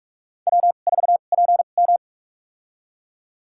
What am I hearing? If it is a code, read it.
Morse code, heard as a keyed tone, about 46 wpm, spaced as for 33 wpm, 700 Hz. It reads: W4PM
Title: USA Call Signs - 4 Characters - 2x Character Spacing - 45wpm